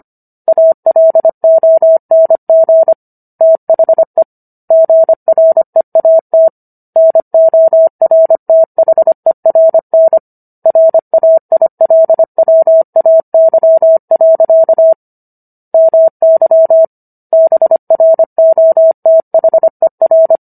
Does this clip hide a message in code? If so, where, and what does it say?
Yes: Morse code from 0.5 to 20.4 s, heard as a keyed tone, about 25 wpm, 650 Hz. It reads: ALONG THE GREAT NORTHERN RAILWAY. MY BROTHER